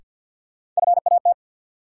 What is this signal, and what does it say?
Morse code, heard as a keyed tone, about 52 wpm, 700 Hz. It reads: FAT
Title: Sentences from Top 600 Words - New Word Drill - 50wpm